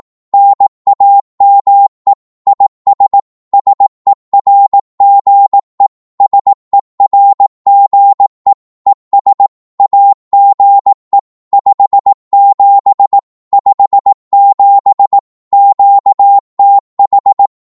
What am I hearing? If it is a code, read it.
Morse code, heard as a keyed tone, about 18 wpm, 800 Hz. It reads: NAME IS SERGE SERGE ES AGE 57 57 QTH
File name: Morse / qso-18wpm